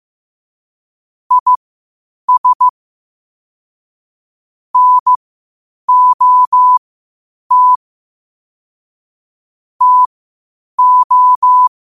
Morse code, heard as a keyed tone, about 15 wpm, spaced as for 8 wpm, 1000 Hz. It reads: IS NOT TO
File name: 8wpm